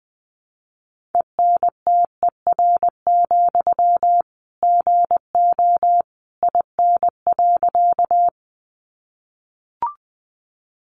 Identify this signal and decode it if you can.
Morse code, heard as a keyed tone, about 20 wpm, 700 Hz. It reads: ENTER, GO IN.
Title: Sentences from Top 1300 Words 20wpm